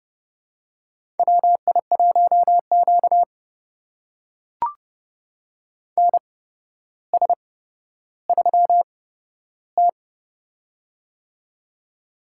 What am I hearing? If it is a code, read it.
Morse code, heard as a keyed tone, about 30 wpm, spaced as for 13 wpm, 700 Hz. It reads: WI1Q NS3T
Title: USA Call Signs - 4 Characters - 8x Character Spacing - 30wpm